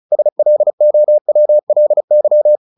Morse code, heard as a keyed tone, about 34 wpm, 600 Hz. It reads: SLOWLY
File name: the old man and the sea 2 35wpm